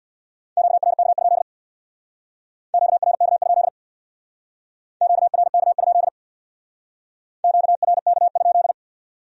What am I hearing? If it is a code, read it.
Morse code, heard as a keyed tone, about 60 wpm, 700 Hz. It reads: QRK? QRK? QRK? QRK?
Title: Speed-Racing Q-Codes - 50wpm